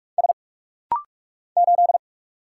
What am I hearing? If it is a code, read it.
Morse code, heard as a keyed tone, about 46 wpm, 700 Hz. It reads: S 8
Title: Single Letter-Number Alt - Rapid-Fire 45wpm